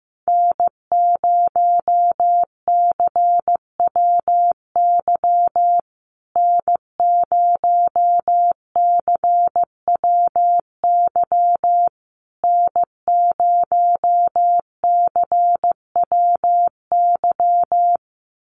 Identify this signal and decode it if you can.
Morse code, heard as a keyed tone, about 15 wpm, 700 Hz. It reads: N0CWY N0CWY N0CWY
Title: USA Call Signs Repeated 3x 15wpm